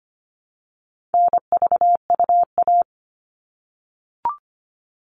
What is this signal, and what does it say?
Morse code, heard as a keyed tone, about 25 wpm, 700 Hz. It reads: NVUA
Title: All Cut Numbers - 4-Digit 25wpm